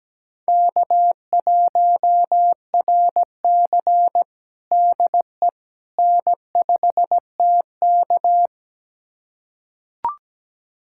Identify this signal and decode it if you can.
Morse code, heard as a keyed tone, about 17 wpm, 700 Hz. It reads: K1RC DE N5TK
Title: USA Callsign DE Callsign - 4 Characters - 17wpm